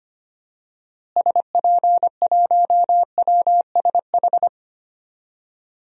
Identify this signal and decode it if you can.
Morse code, heard as a keyed tone, about 25 wpm, 700 Hz. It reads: SP1WSH
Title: DXCC-100 Call Signs - 6 Characters - Repeated 3x - 25wpm